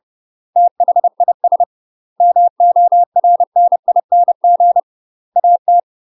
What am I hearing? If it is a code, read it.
Morse code, heard as a keyed tone, about 30 wpm, 700 Hz. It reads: THIS MORNING AT
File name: UK MASTER QSOs TRAINING / 30WPM / UK MASTER QSOs 0025